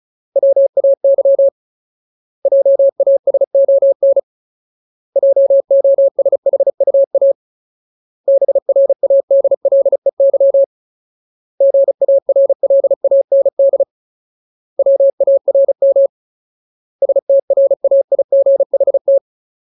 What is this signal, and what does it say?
Morse code, heard as a keyed tone, about 34 wpm, 550 Hz. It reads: WAY JASON JOSHUA BRADLEY GARLAND WARM STRAIGHT
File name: Ultimate Word List / 35WPM Ultimate